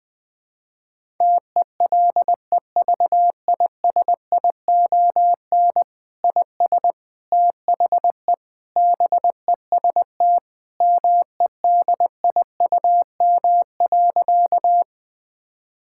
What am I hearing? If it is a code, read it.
Morse code, heard as a keyed tone, about 20 wpm, 700 Hz. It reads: TELEVISION IS THE BEST MEDIUM.
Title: Sentences from Top 1000 Words - Review Part 6 - 20wpm